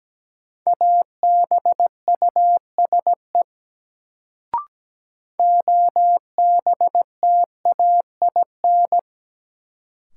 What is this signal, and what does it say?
Morse code, heard as a keyed tone, about 17 wpm, 700 Hz. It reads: ABUSE OBTAIN
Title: Sentences from Top 1300 Words - New Word Drill Alt - 17wpm